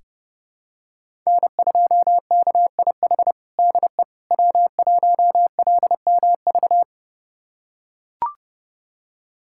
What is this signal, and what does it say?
Morse code, heard as a keyed tone, about 32 wpm, 700 Hz. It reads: N2KIH DE W1LMV
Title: USA Callsign DE Callsign - 5 Characters - 30wpm